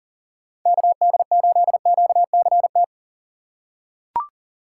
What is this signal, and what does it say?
Morse code, heard as a keyed tone, about 38 wpm, 700 Hz. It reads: KD8QCT
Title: USA Call Signs - 6 Characters - 6x Character Spacing - 40wpm